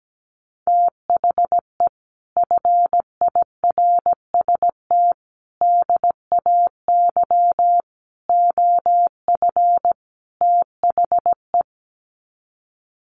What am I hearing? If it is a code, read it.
Morse code, heard as a keyed tone, about 17 wpm, 700 Hz. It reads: THE FIRST DAY OF THE
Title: Sets of 5 Words Encore2 17wpm